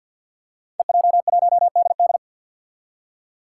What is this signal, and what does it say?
Morse code, heard as a keyed tone, about 52 wpm, 700 Hz. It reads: EJ1DD
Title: Five Challenging Letter-Number Combinations 50wpm